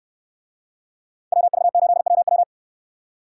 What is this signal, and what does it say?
Morse code, heard as a keyed tone, about 69 wpm, 700 Hz. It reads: WV8WW